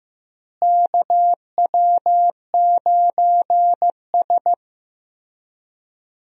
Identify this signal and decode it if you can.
Morse code, heard as a keyed tone, about 15 wpm, 700 Hz. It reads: KW9S
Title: USA Call Signs - 4 Characters - Repeated 2x - 15wpm